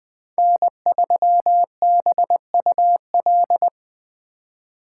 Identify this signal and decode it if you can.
Morse code, heard as a keyed tone, about 21 wpm, 700 Hz. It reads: N3BUL
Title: USA Call Signs - 5 Characters - 2x Character Spacing - 20wpm